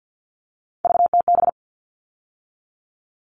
Morse code, heard as a keyed tone, about 48 wpm, 700 Hz. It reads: VTB
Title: All Cut Numbers - 3-Digit 50wpm